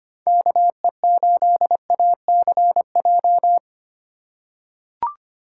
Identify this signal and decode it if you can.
Morse code, heard as a keyed tone, about 25 wpm, 700 Hz. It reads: KE8ACJ
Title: USA Call Signs - 3x Character Spacing - 25wpm